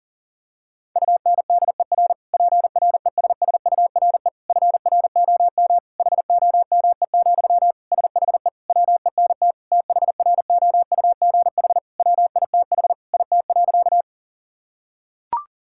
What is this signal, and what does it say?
Morse code, heard as a keyed tone, about 40 wpm, 700 Hz. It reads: UNDER PRESSURE FROM HOME, SHE WENT THROUGH WITH IT.
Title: Sentences from Top 1000 Words - Review Part 6 - 40wpm